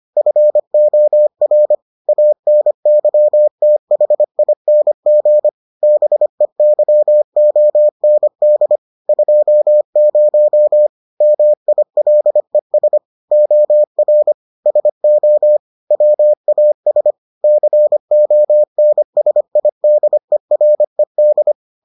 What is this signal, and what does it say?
Morse code, heard as a keyed tone, about 25 wpm, 600 Hz. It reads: FOR ANYTHING BEYOND 20 MILES OR SO WAS CONSIDERED